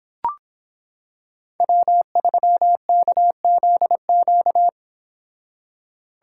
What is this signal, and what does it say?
Morse code, heard as a keyed tone, about 26 wpm, 700 Hz. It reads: W3KZQ